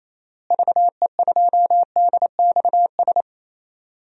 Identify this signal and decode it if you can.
Morse code, heard as a keyed tone, about 28 wpm, 700 Hz. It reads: VE2DXS
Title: DXCC-100 Call Signs - 6 Characters - Repeated 3x - 28wpm